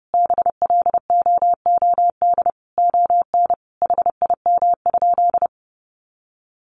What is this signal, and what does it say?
Morse code, heard as a keyed tone, about 30 wpm, 700 Hz. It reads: BLOOD ON HIM?
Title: Sentences from Top 1100 Words 30wpm